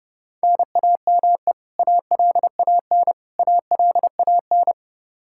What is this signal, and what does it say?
Morse code, heard as a keyed tone, about 32 wpm, 700 Hz. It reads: NAME ALAN ALAN
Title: Practice QSO - Exchange 1 - Beginning - 30wpm